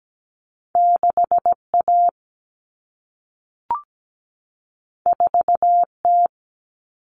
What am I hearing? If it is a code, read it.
Morse code, heard as a keyed tone, about 17 wpm, 700 Hz. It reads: 6A 4T
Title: Common Cut Numbers - 2-Digit 17wpm